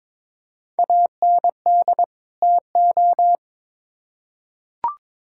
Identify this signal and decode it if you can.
Morse code, heard as a keyed tone, about 21 wpm, 700 Hz. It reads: AND TO
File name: Build A Phrase - 2-Word Phrases - 22wpm